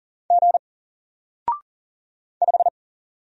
Morse code, heard as a keyed tone, about 39 wpm, 700 Hz. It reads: G 5
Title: Single Character (Letter:Number:Punctuation:Prosign) Alt 40wpm